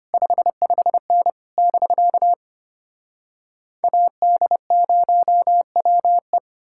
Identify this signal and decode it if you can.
Morse code, heard as a keyed tone, about 33 wpm, 700 Hz. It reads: 55N <BK> AD0WE